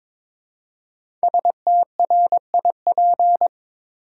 Morse code, heard as a keyed tone, about 22 wpm, 700 Hz. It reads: STRIP
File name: 5 Letter Words - First 3 Letters Given - 22wpm